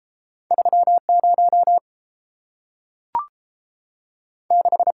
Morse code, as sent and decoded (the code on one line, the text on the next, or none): ...-- / ----- / -....
3 0 6